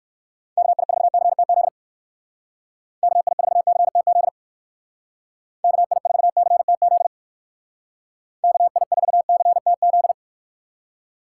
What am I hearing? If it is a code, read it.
Morse code, heard as a keyed tone, about 65 wpm, 700 Hz. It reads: KI4CTZ KI4CTZ KI4CTZ KI4CTZ